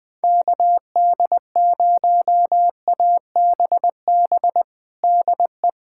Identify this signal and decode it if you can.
Morse code, heard as a keyed tone, about 20 wpm, 700 Hz. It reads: KD0ABB DE
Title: USA Callsign DE Callsign - 6 Characters - 20wpm